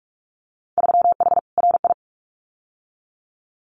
Morse code, heard as a keyed tone, about 46 wpm, 690 Hz. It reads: 3H RI
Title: Winter Field Day Exchanges Encore 45wpm